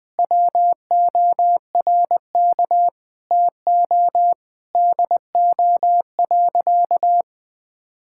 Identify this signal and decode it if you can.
Morse code, heard as a keyed tone, about 21 wpm, 700 Hz. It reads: WORK TO DO.
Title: Sentences with 5-Word Phrases - 20wpm